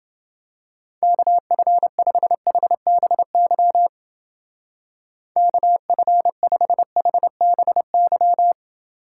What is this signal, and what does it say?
Morse code, heard as a keyed tone, about 30 wpm, 700 Hz. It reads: KF5HBY KF5HBY